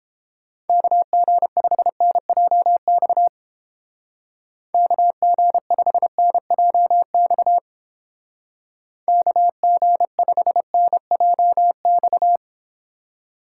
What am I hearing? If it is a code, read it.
Morse code, heard as a keyed tone, about 31 wpm, 700 Hz. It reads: KG5NJX KG5NJX KG5NJX